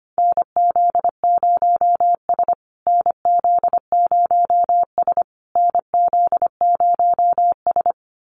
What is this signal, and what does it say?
Morse code, heard as a keyed tone, about 25 wpm, 700 Hz. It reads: NZ0S NZ0S NZ0S
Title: USA Call Signs Repeated 3x 25wpm